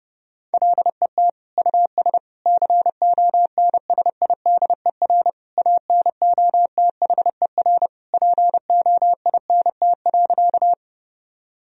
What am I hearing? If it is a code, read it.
Morse code, heard as a keyed tone, about 32 wpm, 700 Hz. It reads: LET US CONSIDER ANOTHER POINT.
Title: Sentences from Top 1000 Words - Review Part 4 - 30wpm